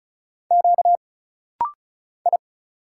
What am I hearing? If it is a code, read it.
Morse code, heard as a keyed tone, about 35 wpm, 700 Hz. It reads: Q I
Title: Single Letter-Number Alt - Rapid-Fire 35wpm